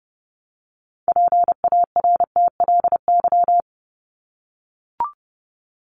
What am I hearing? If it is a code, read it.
Morse code, heard as a keyed tone, about 30 wpm, 700 Hz. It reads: PARTLY
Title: LY Suffix - Top 100 Words with Base Words 30wpm